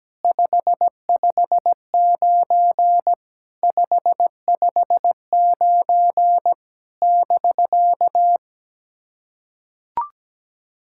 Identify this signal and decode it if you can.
Morse code, heard as a keyed tone, about 17 wpm, 700 Hz. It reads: 559 559 <BK>
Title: SOTA QSO Practice 17wpm